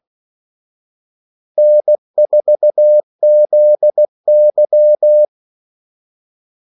Morse code, heard as a keyed tone, about 16 wpm, 600 Hz. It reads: N4ZY
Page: Kaldesignaler 16 wpm